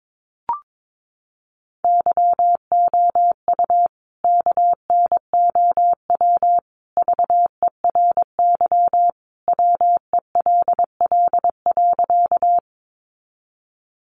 Morse code, heard as a keyed tone, about 22 wpm, 700 Hz. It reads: YOU KNOW VERY WELL.
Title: A Sound Of Thunder 22wpm